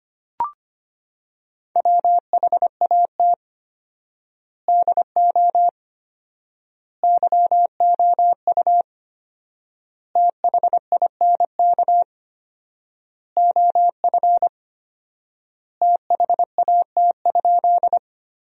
Morse code, heard as a keyed tone, about 25 wpm, 700 Hz. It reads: WHAT DO YOU THINK OF THAT?
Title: 4x Word-Spacing - Sentences from Top 200 Words - 25wpm